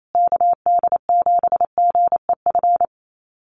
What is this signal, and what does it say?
Morse code, heard as a keyed tone, about 28 wpm, 700 Hz. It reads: KD7GEF
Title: USA Call Signs Repeated 3x 28wpm